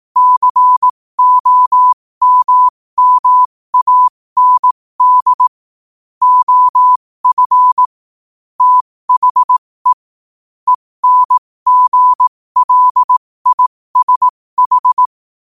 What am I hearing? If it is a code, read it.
Morse code, heard as a keyed tone, about 18 wpm, spaced as for 15 wpm, 1000 Hz. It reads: COMMAND OF THE ENGLISH
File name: qotdfiles / 15wpm